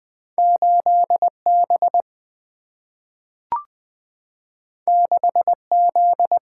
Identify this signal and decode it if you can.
Morse code, heard as a keyed tone, about 20 wpm, 700 Hz. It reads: 8B 6Z
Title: Single-Letter Single-Number 20wpm